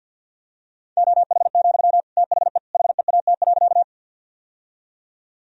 Morse code, heard as a keyed tone, about 50 wpm, 700 Hz. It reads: OH, THE HEAT.
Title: Sentences from Top 1000 Words - Review Part 3 - 50wpm